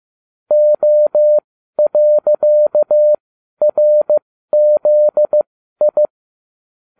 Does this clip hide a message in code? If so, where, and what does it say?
Yes: Morse code from 0.5 to 6.1 s, heard as a keyed tone, about 15 wpm, spaced as for 12 wpm, 600 Hz. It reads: O.RZI